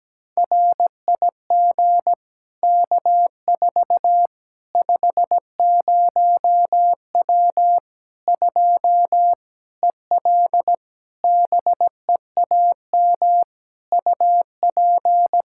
Morse code, heard as a keyed tone, about 17 wpm, 700 Hz. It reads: RIG K4 50W 2 EL BEAM UP
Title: Practice QSO - Exchange 2 - Info - 17wpm